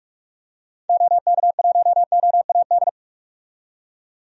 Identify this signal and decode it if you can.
Morse code, heard as a keyed tone, about 46 wpm, 700 Hz. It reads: OK1OAD
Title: DXCC-100 Call Signs - 6 Characters - 45wpm